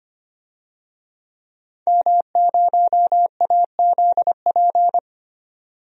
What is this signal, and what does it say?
Morse code, heard as a keyed tone, about 25 wpm, 700 Hz. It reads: M0AZP
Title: DXCC-100 Call Signs - 5 Characters - 25wpm